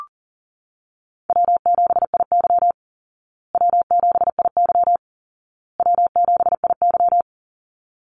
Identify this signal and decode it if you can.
Morse code, heard as a keyed tone, about 40 wpm, 700 Hz. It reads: W7IY W7IY W7IY
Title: Call Signs Repeated 2x - 40wpm